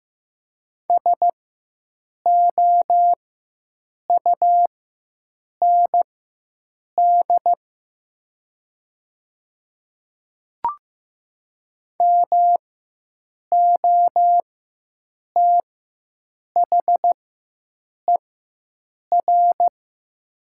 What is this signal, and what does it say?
Morse code, heard as a keyed tone, about 15 wpm, spaced as for 7 wpm, 700 Hz. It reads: SOUND MOTHER
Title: 4x Character Spacing - Top 200 Words - 15wpm